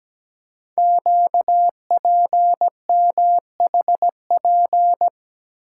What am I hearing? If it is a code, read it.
Morse code, heard as a keyed tone, about 17 wpm, 700 Hz. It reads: QPMHP
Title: Code Groups - 5 Letters - 17wpm